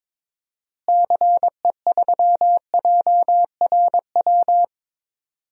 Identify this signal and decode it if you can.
Morse code, heard as a keyed tone, about 22 wpm, 700 Hz. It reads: CE3JRW
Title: DXCC-100 Call Signs - 6 Characters - Repeated 2x - 22wpm